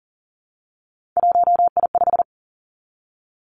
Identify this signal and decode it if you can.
Morse code, heard as a keyed tone, about 42 wpm, 700 Hz. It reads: 1I5
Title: Three Challenging Letter-Number Combinations 40wpm